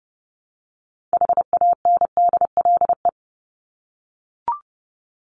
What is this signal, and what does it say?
Morse code, heard as a keyed tone, about 30 wpm, 700 Hz. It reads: HANDLE
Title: Top 1001-1250 Words 30wpm